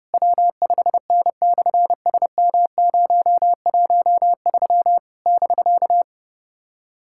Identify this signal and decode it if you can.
Morse code, heard as a keyed tone, about 30 wpm, 690 Hz. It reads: W5N/SM013 <BK>
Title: SOTA QSO Practice 30wpm